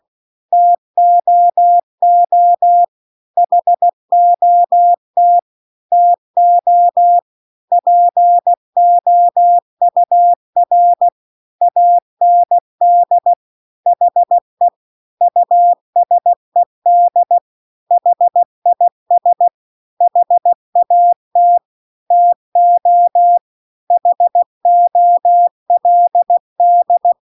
Morse code, heard as a keyed tone, about 16 wpm, 700 Hz. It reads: TOO HOT TO POUR AND HE USED HIS HAT TO HOLD